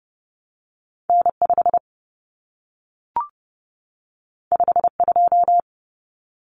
Morse code, heard as a keyed tone, about 30 wpm, 700 Hz. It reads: N5 52
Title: Common Cut Numbers - 2-Digit 30wpm